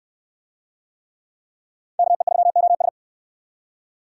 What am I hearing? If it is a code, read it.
Morse code, heard as a keyed tone, about 66 wpm, 700 Hz. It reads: KE3OS